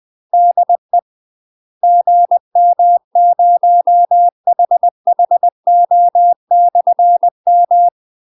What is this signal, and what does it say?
Morse code, heard as a keyed tone, about 20 wpm, 700 Hz. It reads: DE GM0HHO/M
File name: UK MASTER QSOs TRAINING / 20WPM / UK MASTER QSOs 0024